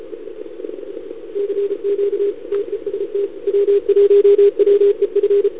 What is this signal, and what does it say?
Morse code, heard as a keyed tone, about 32 wpm, 440 Hz. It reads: CQ TEST W1WEF
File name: CQWWCW2021 / VA2UR